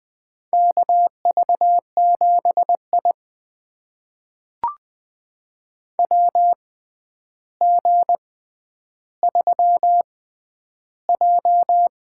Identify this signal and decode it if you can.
Morse code, heard as a keyed tone, about 20 wpm, spaced as for 10 wpm, 700 Hz. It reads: KV7I WG3J
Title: USA Call Signs - 4 Characters - 6x Character Spacing - 20wpm